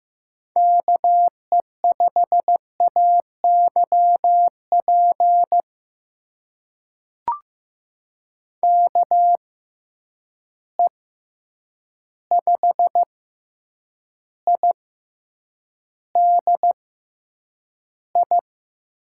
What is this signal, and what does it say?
Morse code, heard as a keyed tone, about 15 wpm, spaced as for 8 wpm, 700 Hz. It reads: KE5AYP KE5IDI
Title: USA Call Signs - 6 Characters - 6x Character Spacing - 15wpm